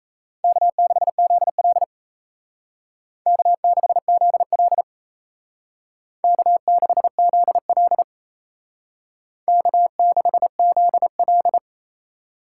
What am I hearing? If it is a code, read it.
Morse code, heard as a keyed tone, about 36 wpm, 700 Hz. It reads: K6ZL K6ZL K6ZL K6ZL